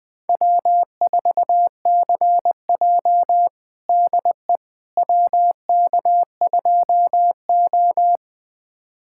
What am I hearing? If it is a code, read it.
Morse code, heard as a keyed tone, about 21 wpm, 700 Hz. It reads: W4CJ DE WK2O
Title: USA Callsign DE Callsign - 4 Characters - 20wpm